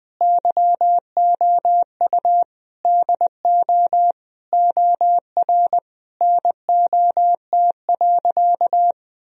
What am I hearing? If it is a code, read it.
Morse code, heard as a keyed tone, about 20 wpm, 700 Hz. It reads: YOU DO OR NOT.
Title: Sentences from Top 1000 Words - Review Part 3 - 20wpm